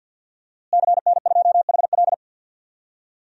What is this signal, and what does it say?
Morse code, heard as a keyed tone, about 52 wpm, 700 Hz. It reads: CN2HL
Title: DXCC-100 Call Signs - 5 Characters - Repeated 3x - 50wpm